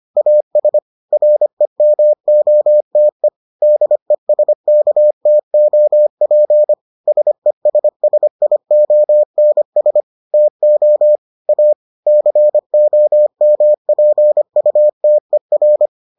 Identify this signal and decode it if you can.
Morse code, heard as a keyed tone, about 25 wpm, 600 Hz. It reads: AS REMOTE DESKTOP SESSIONS TO A COMPUTER